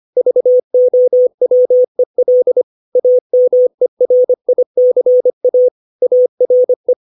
Morse code, heard as a keyed tone, about 25 wpm, 500 Hz. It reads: VOWEL AMERICA ARE
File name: 632 words / 632 Words 25WPM